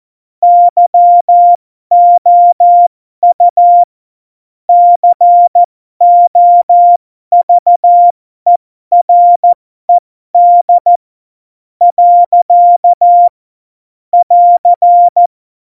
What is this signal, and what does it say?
Morse code, heard as a keyed tone, about 14 wpm, spaced as for 11 wpm, 700 Hz. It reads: YOU COVERED. <AR>
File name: Morsekurs / Witze